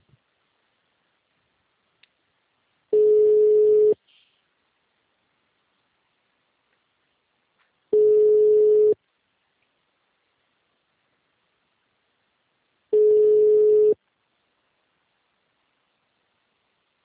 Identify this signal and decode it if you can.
Ringback tone (Europe, 425 Hz).